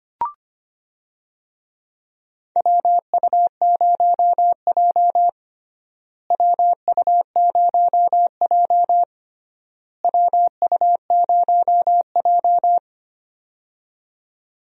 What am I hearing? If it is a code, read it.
Morse code, heard as a keyed tone, about 25 wpm, 700 Hz. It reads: WU0J WU0J WU0J
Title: USA Call Signs - 4 Characters - Repeated 3x - 25wpm